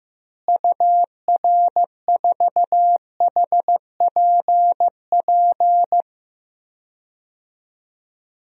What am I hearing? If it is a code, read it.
Morse code, heard as a keyed tone, about 15 wpm, 700 Hz. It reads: UR4HPP
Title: DXCC-100 Call Signs - 6 Characters - Repeated 3x - 15wpm